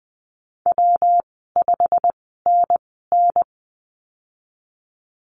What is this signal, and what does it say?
Morse code, heard as a keyed tone, about 20 wpm, 700 Hz. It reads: W 5 N N
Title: 154 Lesson 25g - Letter p - Call Signs 20wpm